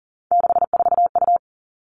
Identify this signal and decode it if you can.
Morse code, heard as a keyed tone, about 42 wpm, 700 Hz. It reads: 64U
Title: Three Challenging Letter-Number Combinations 40wpm